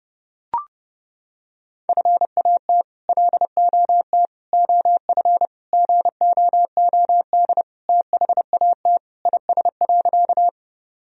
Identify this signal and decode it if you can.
Morse code, heard as a keyed tone, about 30 wpm, 700 Hz. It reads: FAT LOT OF GOOD THAT IS.
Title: Sentences from Top 1000 Words - Review Part 4 - 30wpm